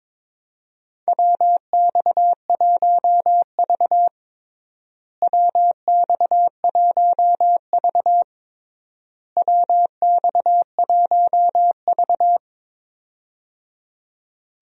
Morse code, heard as a keyed tone, about 22 wpm, 700 Hz. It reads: WX1V WX1V WX1V
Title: USA Call Signs - 4 Characters - Repeated 3x - 22wpm